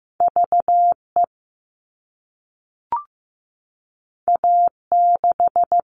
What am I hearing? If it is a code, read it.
Morse code, heard as a keyed tone, about 15 wpm, 700 Hz. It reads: VE A6
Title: All Cut Numbers - 2-Digit 15wpm